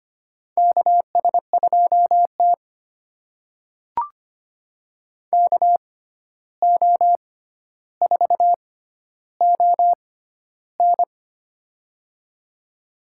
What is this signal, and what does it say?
Morse code, heard as a keyed tone, about 25 wpm, spaced as for 12 wpm, 700 Hz. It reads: KS2T KO4ON
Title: USA Call Signs - 6x Character Spacing - 25wpm